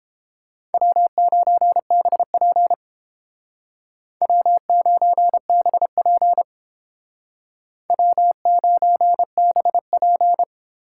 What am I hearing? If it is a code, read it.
Morse code, heard as a keyed tone, about 29 wpm, 700 Hz. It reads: W9BP W9BP W9BP